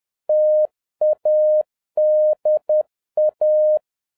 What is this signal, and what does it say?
Morse code, heard as a keyed tone, about 10 wpm, 610 Hz. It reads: TADA